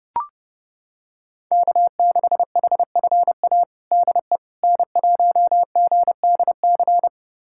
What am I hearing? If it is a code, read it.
Morse code, heard as a keyed tone, about 32 wpm, 700 Hz. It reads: K6HFA DE N1GDC
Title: USA Callsign DE Callsign - 5 Characters - 30wpm